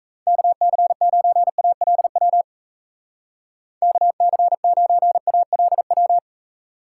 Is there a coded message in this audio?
Yes: Morse code from 0.3 to 6.2 s, heard as a keyed tone, about 37 wpm, 700 Hz. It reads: KC9ALW KC9ALW